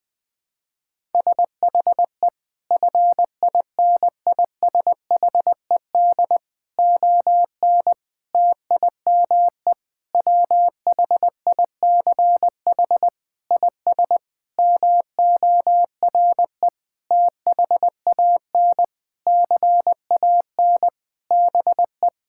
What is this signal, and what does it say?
Morse code, heard as a keyed tone, about 20 wpm, 700 Hz. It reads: SHE FINISHED ON TIME WHICH IS MORE THAN CAN BE
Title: Sentences with 8-Word Phrases - Spoken Beginning - 20wpm